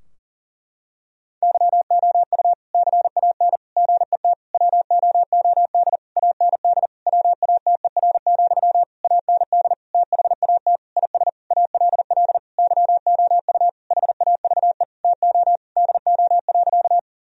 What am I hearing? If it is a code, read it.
Morse code, heard as a keyed tone, about 40 wpm, 700 Hz. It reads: YOU CAN GET WOOD AND WATER, AND THAT IS ALL YOU HAVE TO DO.
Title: Sentences from Top 1000 Words - Review Part 2 - 40wpm